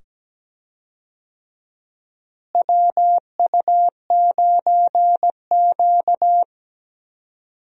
Morse code, heard as a keyed tone, about 17 wpm, 700 Hz. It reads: WU9Q
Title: USA Call Signs - 4 Characters - Repeated 2x - 17wpm